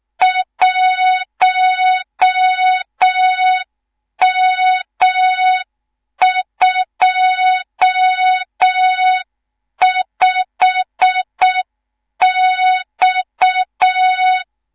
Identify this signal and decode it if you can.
Morse code, heard as a keyed tone, about 6 wpm, 3700 Hz. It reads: 1M25X
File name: Morze-HA2MN / hang / 18 IRASJELEK2